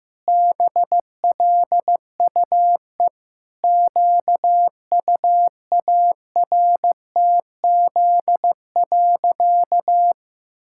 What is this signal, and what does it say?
Morse code, heard as a keyed tone, about 16 wpm, 700 Hz. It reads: BLUE QUARTZ.
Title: Pangram Challenge v2 - 15wpm